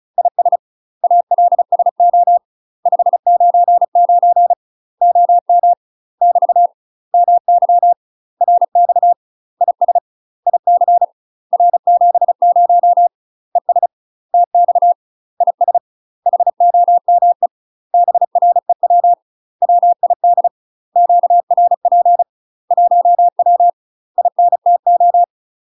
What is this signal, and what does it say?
Morse code, heard as a keyed tone, about 35 wpm, 700 Hz. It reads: IS ALSO 599 OM <BT> MY RX IS IC R70 ES TX IS HOME BREW WID QRP 1W INTO
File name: UK MASTER QSOs TRAINING / 35WPM / UK MASTER QSOs 0004